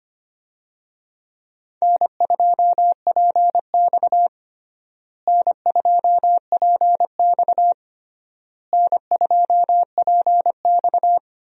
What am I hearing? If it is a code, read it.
Morse code, heard as a keyed tone, about 25 wpm, 700 Hz. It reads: N2PX N2PX N2PX
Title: USA Call Signs - 4 Characters - Repeated 3x - 25wpm